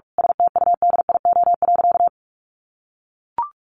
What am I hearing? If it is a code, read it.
Morse code, heard as a keyed tone, about 44 wpm, 700 Hz. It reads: STUDIO.
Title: Sentences from Top 1200 Words 45wpm